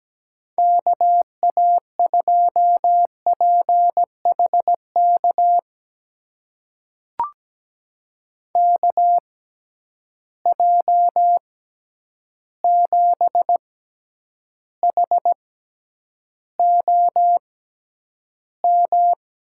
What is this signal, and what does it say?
Morse code, heard as a keyed tone, about 17 wpm, spaced as for 9 wpm, 700 Hz. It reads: KA2PHK KJ7HOM
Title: USA Call Signs - 6 Characters - 6x Character Spacing - 17wpm